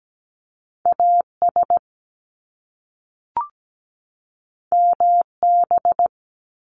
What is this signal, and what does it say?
Morse code, heard as a keyed tone, about 17 wpm, 700 Hz. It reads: AS MB
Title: Top Two Letter Combinations in Words 17wpm